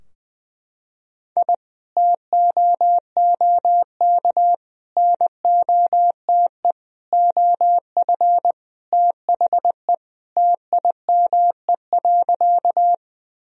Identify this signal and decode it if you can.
Morse code, heard as a keyed tone, about 20 wpm, 700 Hz. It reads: I TOOK NOTE OF THE TIME.
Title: Sentences from Top 1000 Words - Review Part 2 - 20wpm